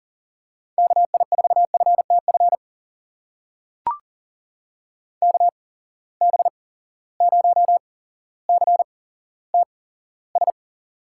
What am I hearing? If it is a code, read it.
Morse code, heard as a keyed tone, about 40 wpm, spaced as for 17 wpm, 700 Hz. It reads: KI4FTF KB0CTS
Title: USA Call Signs - 6 Characters - 8x Character Spacing - 40wpm